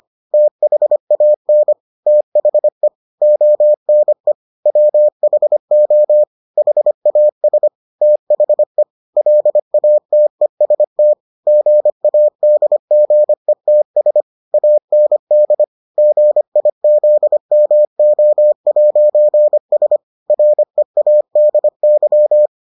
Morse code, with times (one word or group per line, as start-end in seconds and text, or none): 0.3-1.7 THAN
2.1-2.9 THE
3.2-4.3 ONE
4.7-6.2 WHO
6.6-7.7 HAS
8.0-8.8 THE
9.2-11.1 LATEST
11.5-14.2 GADGETS
14.5-15.6 AND
16.0-20.0 GIZMO'S
20.3-22.5 READY